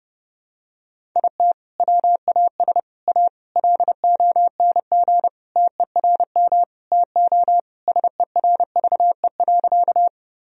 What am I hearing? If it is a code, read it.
Morse code, heard as a keyed tone, about 28 wpm, 700 Hz. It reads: IT WAS A LONG TERM TO SERVE.
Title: Sentences from Top 1000 Words - Review Part 4 - 30wpm